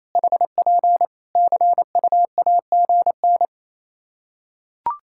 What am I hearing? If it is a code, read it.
Morse code, heard as a keyed tone, about 28 wpm, 700 Hz. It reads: HP CUAGN
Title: Practice QSO - Exchange 4 - Ending - 28wpm